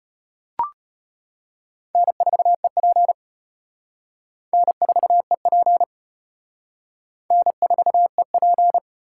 Morse code, heard as a keyed tone, about 30 wpm, 700 Hz. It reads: N4EP N4EP N4EP
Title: Speed Racing - USA 1x2 Call Signs 25wpm